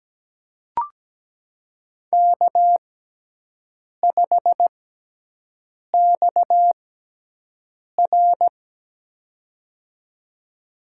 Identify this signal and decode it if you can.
Morse code, heard as a keyed tone, about 17 wpm, spaced as for 6 wpm, 700 Hz. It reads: K5XR
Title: USA Call Signs - 4 Characters - 6x Character Spacing - 17wpm